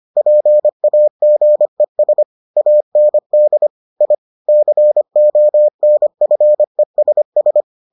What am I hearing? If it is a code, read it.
Morse code, heard as a keyed tone, about 25 wpm, 600 Hz. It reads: PAGES AND I CONFESS